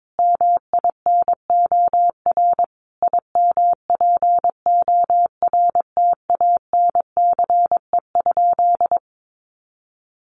Morse code, heard as a keyed tone, about 22 wpm, 700 Hz. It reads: MINOR IMPORTANCE?
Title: Sentences from Top 1500 Words 22wpm